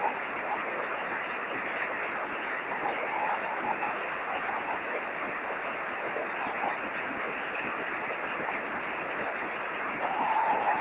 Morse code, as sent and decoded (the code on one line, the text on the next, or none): none